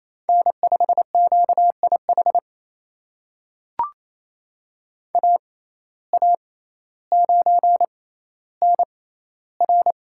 Morse code, heard as a keyed tone, about 28 wpm, spaced as for 14 wpm, 700 Hz. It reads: N5QIH AA9NR
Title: USA Call Signs - 5 Characters - 6x Character Spacing - 28wpm